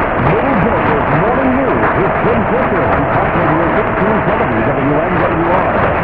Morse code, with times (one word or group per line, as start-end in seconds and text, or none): none